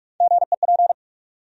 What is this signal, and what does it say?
Morse code, heard as a keyed tone, about 47 wpm, 700 Hz. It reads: GEP
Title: USA Call Sign Postfixes - 3 Letters 45wpm